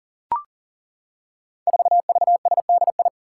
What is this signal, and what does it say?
Morse code, heard as a keyed tone, about 40 wpm, 700 Hz. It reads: 4VSDI
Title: Five Challenging Letter-Number Combinations 40wpm